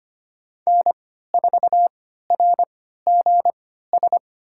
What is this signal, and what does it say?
Morse code, heard as a keyed tone, about 25 wpm, spaced as for 14 wpm, 700 Hz. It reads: N4RGS
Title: USA Call Signs - 5 Characters - 3x Character Spacing - 25wpm